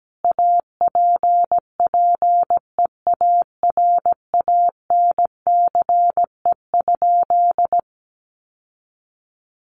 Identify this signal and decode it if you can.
Morse code, heard as a keyed tone, about 17 wpm, 700 Hz. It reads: APPEARANCE?
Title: Sentences from Top 1100 Words 17wpm